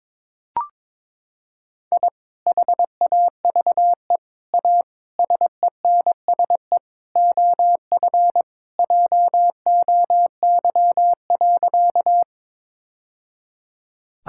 Morse code, heard as a keyed tone, about 22 wpm, 700 Hz. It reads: I HAVE A SENSE OF JOY.
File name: practice / 22wpm / Sentences with 3-Word Phrases - 22wpm